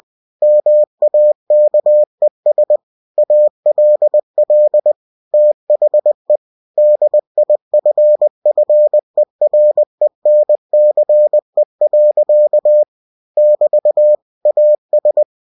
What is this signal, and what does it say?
Morse code, heard as a keyed tone, about 20 wpm, 600 Hz. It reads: MAKES ALL THE DIFFERENCE. <BT> AS
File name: cw-books / winnie-the-pooh / winnie-20wpm-page-0084